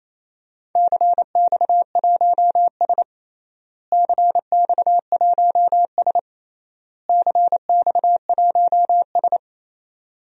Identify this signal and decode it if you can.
Morse code, heard as a keyed tone, about 28 wpm, 700 Hz. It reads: CX1S CX1S CX1S
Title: DXCC-100 Call Signs - 4 Characters - Repeated 3x - 28wpm